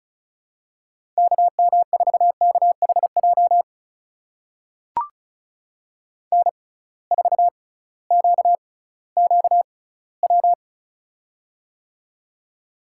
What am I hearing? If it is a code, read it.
Morse code, heard as a keyed tone, about 36 wpm, spaced as for 19 wpm, 700 Hz. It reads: KM4KHJ N4QQW
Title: USA Call Signs - 6x Character Spacing - 35wpm